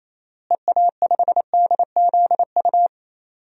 Morse code, heard as a keyed tone, about 28 wpm, 700 Hz. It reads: EA5DZU
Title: DXCC-100 Call Signs - 6 Characters - Repeated 3x - 28wpm